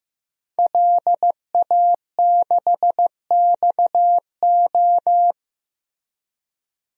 Morse code, heard as a keyed tone, about 15 wpm, 700 Hz. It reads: LA6XO
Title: DXCC-100 Call Signs - 5 Characters - Repeated 3x - 15wpm